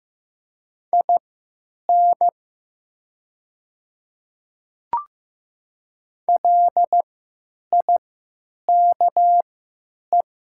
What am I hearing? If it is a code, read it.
Morse code, heard as a keyed tone, about 15 wpm, spaced as for 8 wpm, 700 Hz. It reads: IN LIKE
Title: 3x Character Spacing - Top 100 Words - 15wpm